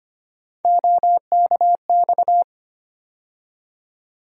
Morse code, heard as a keyed tone, about 25 wpm, 700 Hz. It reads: OKX
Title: USA Call Sign Postfixes - 3 Letters 25wpm